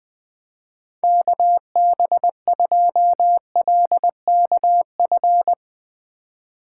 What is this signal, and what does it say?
Morse code, heard as a keyed tone, about 22 wpm, 700 Hz. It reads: KB2LKF
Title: Speed Racing - USA 2x3 Call Signs 20wpm